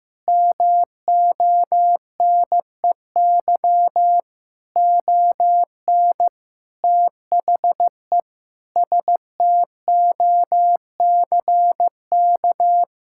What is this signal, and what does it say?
Morse code, heard as a keyed tone, about 15 wpm, 700 Hz. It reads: MONEY ON THE STOCK
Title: Sentences from Top 1000 Words - Review Part 6 - 15wpm